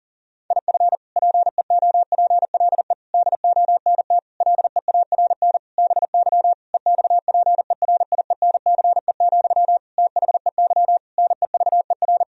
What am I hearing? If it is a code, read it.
Morse code, heard as a keyed tone, about 37 wpm, 700 Hz. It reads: IF PEOPLE DONT LEARN BY EXPERIENCE, THEY NEVER
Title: Sentences from Top 1000 Words - Review Part 3 - 40wpm